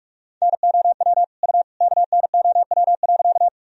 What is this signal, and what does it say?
Morse code, heard as a keyed tone, about 46 wpm, 700 Hz. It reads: NOW U KNOW.
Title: Sentences from Top 100 Words and Abbreviations 45wpm